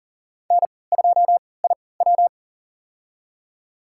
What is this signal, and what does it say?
Morse code, heard as a keyed tone, about 40 wpm, spaced as for 23 wpm, 700 Hz. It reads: N2IW
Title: USA Call Signs - 4 Characters - 3x Character Spacing - 40wpm